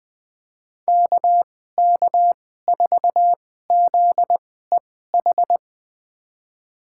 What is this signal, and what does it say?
Morse code, heard as a keyed tone, about 20 wpm, spaced as for 14 wpm, 700 Hz. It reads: KK4ZEH
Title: USA Call Signs - 6 Characters - 2x Character Spacing - 20wpm